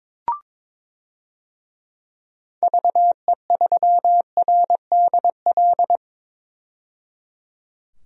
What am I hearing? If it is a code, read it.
Morse code, heard as a keyed tone, about 21 wpm, 700 Hz. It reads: VE3RDL
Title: DXCC-100 Call Signs - 6 Characters - 22wpm